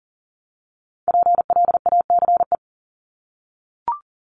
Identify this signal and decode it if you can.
Morse code, heard as a keyed tone, about 40 wpm, 700 Hz. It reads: PLACE
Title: Top 500 Words 40wpm